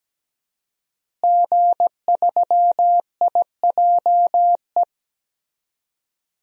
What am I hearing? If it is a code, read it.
Morse code, heard as a keyed tone, about 17 wpm, 700 Hz. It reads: G3IJE
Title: DXCC-100 Call Signs - 5 Characters - Repeated 2x - 17wpm